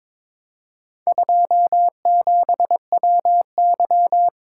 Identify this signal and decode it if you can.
Morse code, heard as a keyed tone, about 22 wpm, 700 Hz. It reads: 27WY
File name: practice / 22wpm / Four Letter-Number Combinations - 22wpm